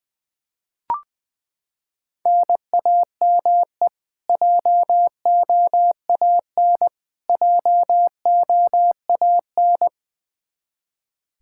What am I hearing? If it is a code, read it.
Morse code, heard as a keyed tone, about 21 wpm, 700 Hz. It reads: NAME JOAN JOAN
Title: Practice QSO - Exchange 1 - Beginning - 20wpm